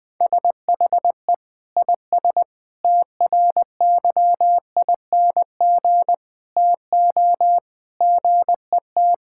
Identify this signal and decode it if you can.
Morse code, heard as a keyed tone, about 21 wpm, 700 Hz. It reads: SHE IS TRYING TO GET
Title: Sentences with 3-Word Phrases - 20wpm